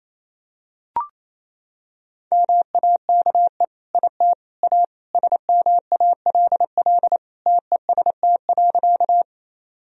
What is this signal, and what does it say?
Morse code, heard as a keyed tone, about 28 wpm, 700 Hz. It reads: MAKE IT A SMALL TEST.
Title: Sentences from Top 1000 Words - Review Part 3 - 28wpm